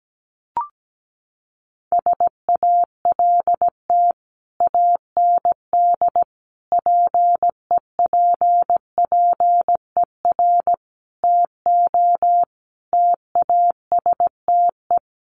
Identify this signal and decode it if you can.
Morse code, heard as a keyed tone, about 17 wpm, 700 Hz. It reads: SALT AND PEPPER TO TASTE
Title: Sets of 5 Words 17wpm